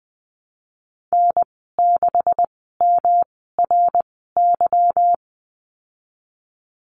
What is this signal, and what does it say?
Morse code, heard as a keyed tone, about 20 wpm, spaced as for 15 wpm, 700 Hz. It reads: N6MRY
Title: 147 Lesson 24g - Letter y - Call Signs 20wpm